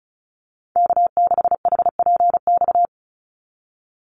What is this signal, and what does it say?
Morse code, heard as a keyed tone, about 36 wpm, 700 Hz. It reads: K6HPX
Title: NA SOTA Activator Callsigns 35wpm